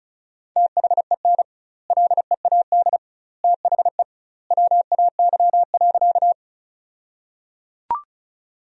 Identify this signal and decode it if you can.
Morse code, heard as a keyed tone, about 34 wpm, 700 Hz. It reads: THEN LEAD THE WAY.
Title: 2x Word-Spacing - Sentences from Top 500 Words - 35wpm